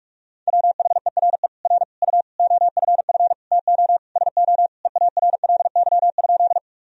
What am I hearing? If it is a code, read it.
Morse code, heard as a keyed tone, about 41 wpm, 700 Hz. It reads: WHERE R U OFF TO SO EARLY?
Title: Sentences from Top 400 Words and Abbreviations 45wpm